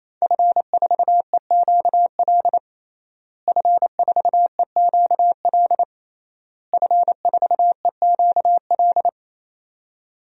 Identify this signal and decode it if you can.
Morse code, heard as a keyed tone, about 28 wpm, 700 Hz. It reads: F4EQL F4EQL F4EQL
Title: DXCC-100 Call Signs - 5 Characters - Repeated 3x - 28wpm